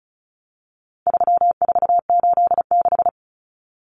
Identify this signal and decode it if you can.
Morse code, heard as a keyed tone, about 34 wpm, 700 Hz. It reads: 3486
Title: Common Cut Numbers - 4-Digit 35wpm